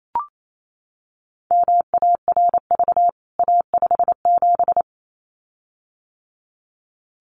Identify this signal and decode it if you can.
Morse code, heard as a keyed tone, about 28 wpm, 700 Hz. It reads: MARV A57
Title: CWT Exchange 28wpm